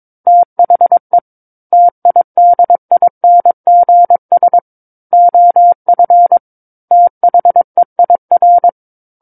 Morse code, heard as a keyed tone, about 22 wpm, 700 Hz. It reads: THE TIDINGS OF THEIR
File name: Book One / 13-Chapter 13 - 22 WPM